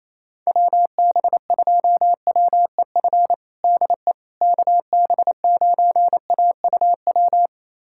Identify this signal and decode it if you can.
Morse code, heard as a keyed tone, about 27 wpm, 700 Hz. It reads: WB2WEF DE KB9AUW
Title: USA Callsign DE Callsign 28wpm